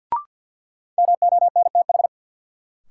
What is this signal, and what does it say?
Morse code, heard as a keyed tone, about 53 wpm, 700 Hz. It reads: MONTH
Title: Sentences from Top 800 Words - New Word Drill - 50wpm